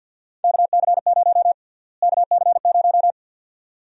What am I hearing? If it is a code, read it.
Morse code, heard as a keyed tone, about 48 wpm, 700 Hz. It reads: KC0 KC0
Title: USA Call Sign Prefixes - Repeated 2x - 50wpm